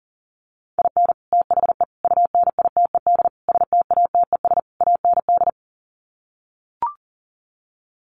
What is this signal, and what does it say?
Morse code, heard as a keyed tone, about 40 wpm, 700 Hz. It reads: IN THE UNITED STATES AND
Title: Sets of 5 Words 40wpm